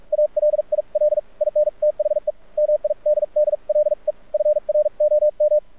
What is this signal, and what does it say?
Morse code, heard as a keyed tone, about 46 wpm, spaced as for 39 wpm, 600 Hz, with radio noise. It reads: ALIL IN THE MIDDLE FROM
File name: archived / CRSnet